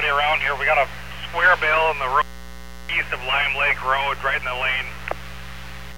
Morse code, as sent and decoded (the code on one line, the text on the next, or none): none